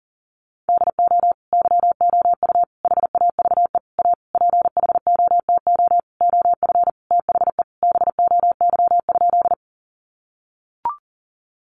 Morse code, heard as a keyed tone, about 40 wpm, 700 Hz. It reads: DO YOU HAVE A PHOTO OF THE BOY?
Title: Sentences from Top 1200 Words 40wpm